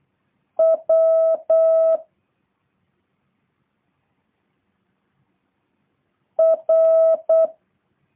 Morse code, heard as a keyed tone, about 8 wpm, 650 Hz. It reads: W R